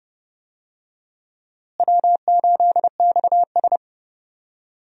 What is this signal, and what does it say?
Morse code, heard as a keyed tone, about 30 wpm, 700 Hz. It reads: W8XS
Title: USA Call Signs - 4 Characters - Repeated 3x - 30wpm